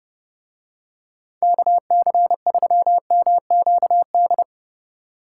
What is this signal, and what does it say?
Morse code, heard as a keyed tone, about 29 wpm, 700 Hz. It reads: KC3MQD